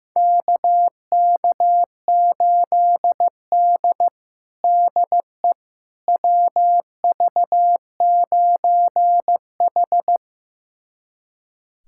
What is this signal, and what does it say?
Morse code, heard as a keyed tone, about 15 wpm, 700 Hz. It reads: KK8D DE WV9H
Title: USA Callsign DE Callsign - 4 Characters - 15wpm